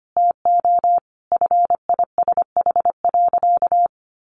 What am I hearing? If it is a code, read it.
Morse code, heard as a keyed tone, about 25 wpm, 700 Hz. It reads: TO FISH.
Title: Sentences from Top 300 Words 25wpm